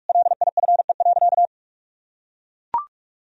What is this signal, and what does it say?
Morse code, heard as a keyed tone, about 41 wpm, 700 Hz. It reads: LIFE.
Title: Sentences from Top 1000 Words - Review Part 2 - 45wpm